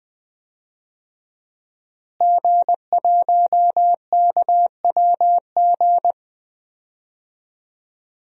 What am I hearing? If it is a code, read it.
Morse code, heard as a keyed tone, about 20 wpm, 700 Hz. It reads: G1KWG
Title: DXCC-100 Call Signs - 5 Characters - Repeated 2x - 20wpm